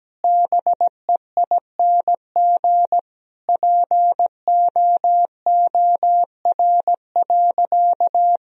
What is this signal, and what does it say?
Morse code, heard as a keyed tone, about 17 wpm, 700 Hz. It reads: BEING POOR.
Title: Sentences from Top 1000 Words - Review Part 3 - 17wpm